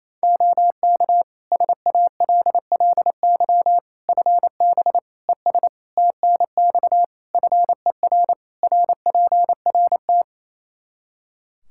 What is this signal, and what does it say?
Morse code, heard as a keyed tone, about 28 wpm, 700 Hz. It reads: OK SALLY FB ES TNX FER RPRT
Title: Practice QSO - Exchange 2 - Info - 28wpm